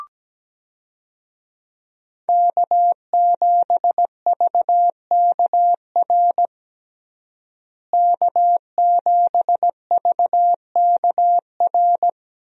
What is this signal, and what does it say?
Morse code, heard as a keyed tone, about 17 wpm, 700 Hz. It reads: K7VKR K7VKR
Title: USA Call Signs - 5 Characters - Repeated 2x - 17wpm